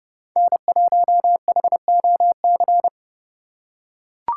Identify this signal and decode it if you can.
Morse code, heard as a keyed tone, about 31 wpm, 700 Hz. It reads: N1HOC
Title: USA Call Signs - 5 Characters - 3x Character Spacing - 30wpm